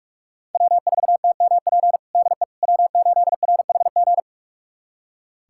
Morse code, heard as a keyed tone, about 43 wpm, 700 Hz. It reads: W4TMP DE W8RHG
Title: USA Callsign DE Callsign - 5 Characters - 45wpm